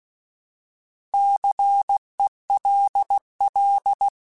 Morse code, heard as a keyed tone, about 16 wpm, 800 Hz. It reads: CELL
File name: Alphabetical / Alphabetical mp3s / Cs @15